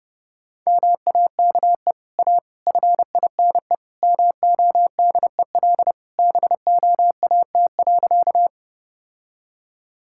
Morse code, heard as a keyed tone, about 29 wpm, 700 Hz. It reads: MAKE A FINE MODEL BOAT.
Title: Sentences from Top 1000 Words - Review Part 5 - 30wpm